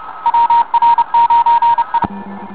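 Morse code, heard as a keyed tone, about 32 wpm, 940 Hz, with radio noise. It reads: WR9E
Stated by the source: CW contest recording RM40 for contest training, with N4PN, VY2ZM, N4BP, W2TX, K4TS and partial calls.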